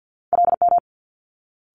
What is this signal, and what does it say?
Morse code, heard as a keyed tone, about 50 wpm, 700 Hz. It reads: LM
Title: Multi-letter Phonemes Quick Fire 50wpm